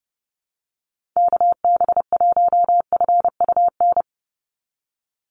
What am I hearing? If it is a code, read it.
Morse code, heard as a keyed tone, about 32 wpm, 700 Hz. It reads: KB1FUN
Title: USA Call Signs 30wpm